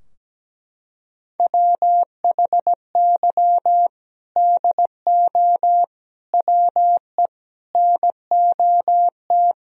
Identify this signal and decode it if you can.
Morse code, heard as a keyed tone, about 17 wpm, 700 Hz. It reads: WHY DO WE NOT
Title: Sentences from Top 1000 Words - Review Part 3 - 17wpm